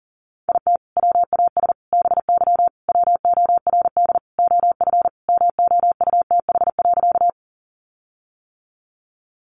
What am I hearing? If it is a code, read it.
Morse code, heard as a keyed tone, about 40 wpm, 700 Hz. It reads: IT WAS BY WORD OF MOUTH.
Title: Sentences from Top 1500 Words 40wpm